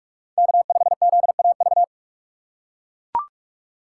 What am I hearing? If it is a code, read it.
Morse code, heard as a keyed tone, about 42 wpm, 690 Hz. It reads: K5ZAV
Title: USA Call Signs - 5 Characters - 2x Character Spacing - 45wpm